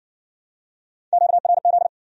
Morse code, heard as a keyed tone, about 56 wpm, 700 Hz. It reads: QRZ